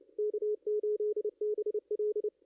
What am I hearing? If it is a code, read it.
Morse code, heard as a keyed tone, about 29 wpm, 420 Hz, with radio noise. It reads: K8BL
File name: contests / ss-cw-k5tr-op-n5rz -2300